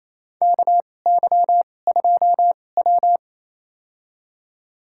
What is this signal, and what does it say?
Morse code, heard as a keyed tone, about 27 wpm, spaced as for 20 wpm, 700 Hz. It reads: KY2W
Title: USA Call Signs - 4 Characters - 2x Character Spacing - 28wpm